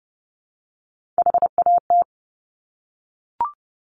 Morse code, heard as a keyed tone, about 30 wpm, 700 Hz. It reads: HAT
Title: Top Three Letter Combinations in Words 30wpm